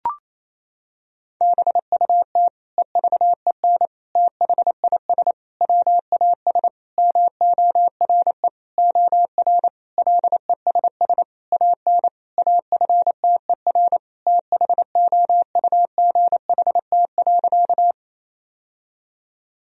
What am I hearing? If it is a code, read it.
Morse code, heard as a keyed tone, about 28 wpm, 700 Hz. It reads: BUT EVEN THIS WAS MORE OR LESS AN AFTER THOUGHT.
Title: Sentences from Top 1000 Words - Review Part 2 - 28wpm